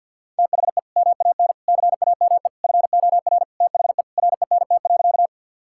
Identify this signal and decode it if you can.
Morse code, heard as a keyed tone, about 51 wpm, 700 Hz. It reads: THE MAN CAME FOR THE RENT.
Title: Sentences from Top 1000 Words - Review Part 5 - 50wpm